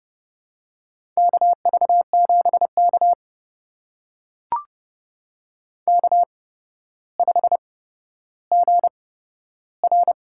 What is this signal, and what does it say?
Morse code, heard as a keyed tone, about 30 wpm, spaced as for 13 wpm, 700 Hz. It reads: KV7K K5GR
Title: USA Call Signs - 4 Characters - 8x Character Spacing - 30wpm